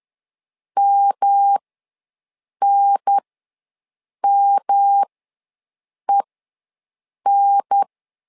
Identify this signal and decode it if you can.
Morse code, heard as a keyed tone, about 10 wpm, spaced as for 6 wpm, 800 Hz. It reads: MNMEN
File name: LEZIONI / 3 MWRO / 3 M ws 10wpm 2121